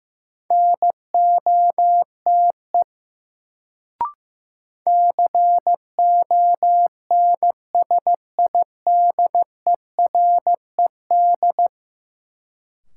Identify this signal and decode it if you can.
Morse code, heard as a keyed tone, about 15 wpm, 700 Hz. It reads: NOTE CONSIDERED
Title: Sets of 5 Words - New Word Drill - 15wpm